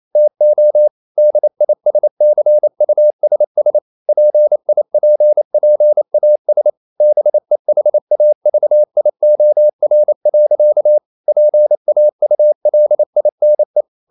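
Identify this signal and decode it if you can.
Morse code, heard as a keyed tone, about 28 wpm, 600 Hz. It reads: TO DISCUSS PIPPAS BEHAVIOR. PAULINE
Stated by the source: Morse code practice at 28 wpm, text of Ken Follett's novel Never in English, from CW med Gnister.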